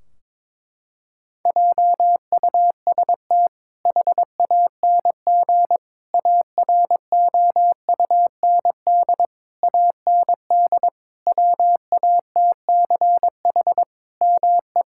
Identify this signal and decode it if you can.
Morse code, heard as a keyed tone, about 22 wpm, 700 Hz. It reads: JUST HANG AROUND AND WATCH ME
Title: Sentences from Top 1600 Words 22wpm